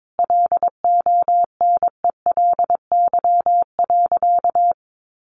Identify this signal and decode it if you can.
Morse code, heard as a keyed tone, about 22 wpm, 700 Hz. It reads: LONELY.
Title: Easy Phrases 22wpm